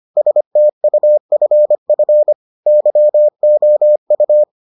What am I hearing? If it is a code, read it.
Morse code, heard as a keyed tone, about 25 wpm, 600 Hz. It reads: STUFF YOU